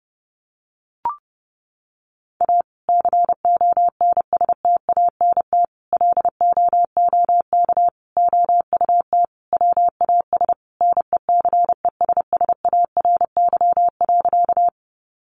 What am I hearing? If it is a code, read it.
Morse code, heard as a keyed tone, about 30 wpm, 700 Hz. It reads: A CONSTANT LOOK OUT WAS NECESSARY.
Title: Sentences from Top 1500 Words 30wpm